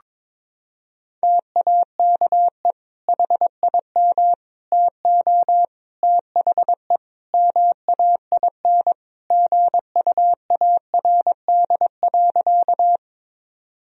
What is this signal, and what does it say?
Morse code, heard as a keyed tone, about 22 wpm, 700 Hz. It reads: TAKE HIM TO THE MAIN GUARD.
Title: Sentences from Top 1000 Words - Review Part 4 - 22wpm